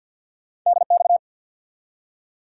Morse code, heard as a keyed tone, about 53 wpm, 700 Hz. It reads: DX
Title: Top 90 QSO Elements 50wpm